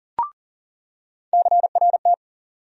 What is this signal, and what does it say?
Morse code, heard as a keyed tone, about 37 wpm, 700 Hz. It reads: CRT
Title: USA Call Sign Postfixes - 3 Letters 40wpm